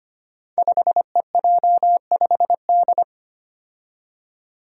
Morse code, heard as a keyed tone, about 25 wpm, 700 Hz. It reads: 5EJ5D
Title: Five Challenging Letter-Number Combinations 25wpm